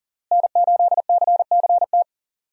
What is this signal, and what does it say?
Morse code, heard as a keyed tone, about 40 wpm, 700 Hz. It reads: N8CCT
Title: USA Call Signs - 5 Characters - Repeated 3x - 40wpm